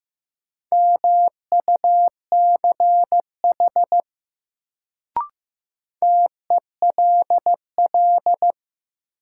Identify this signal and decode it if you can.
Morse code, heard as a keyed tone, about 15 wpm, 700 Hz. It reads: MUCH TELL
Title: Sentences from Top 200 Words - New Word Drill - 15wpm